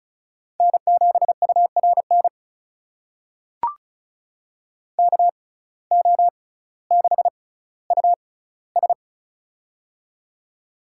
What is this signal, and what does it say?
Morse code, heard as a keyed tone, about 36 wpm, spaced as for 18 wpm, 700 Hz. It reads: N7URN KO6US
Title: USA Call Signs - 6x Character Spacing - 35wpm